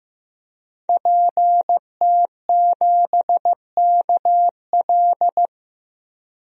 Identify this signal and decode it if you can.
Morse code, heard as a keyed tone, about 15 wpm, 700 Hz. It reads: PT7KL
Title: DXCC-100 Call Signs - 5 Characters - 15wpm